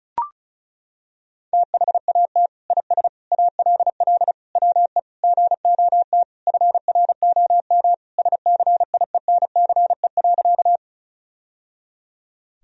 Morse code, heard as a keyed tone, about 36 wpm, 700 Hz. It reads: THAT IS ALL WE GOT FROM SCIENCE.
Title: Sentences from Top 1000 Words - Review Part 2 - 35wpm